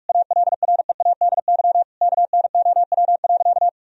Morse code, heard as a keyed tone, about 44 wpm, 700 Hz. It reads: ALREADY KNOW.
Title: Sentences from Top 1000 Words - Review Part 1 - 45wpm